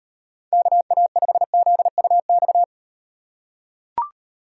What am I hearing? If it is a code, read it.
Morse code, heard as a keyed tone, about 38 wpm, 700 Hz. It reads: KA5ZUX